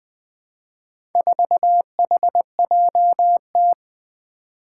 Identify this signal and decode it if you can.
Morse code, heard as a keyed tone, about 20 wpm, 700 Hz. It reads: 4HJT
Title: Four Challenging Letter-Number Combinations 20wpm